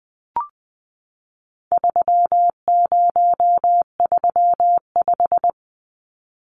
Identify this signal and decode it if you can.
Morse code, heard as a keyed tone, about 20 wpm, 700 Hz. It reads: 3035
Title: Numbers - 4-Digit 20wpm